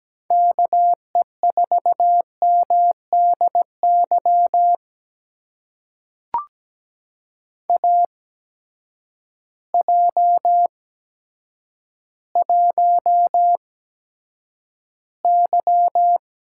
Morse code, heard as a keyed tone, about 17 wpm, spaced as for 9 wpm, 700 Hz. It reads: KE4MDY AJ1Y
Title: USA Call Signs - 8x Character Spacing - 17wpm